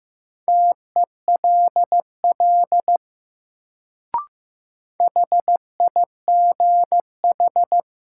Morse code, heard as a keyed tone, about 15 wpm, 700 Hz. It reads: TELL HIGH
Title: Sentences from Top 200 Words - New Word Drill - 15wpm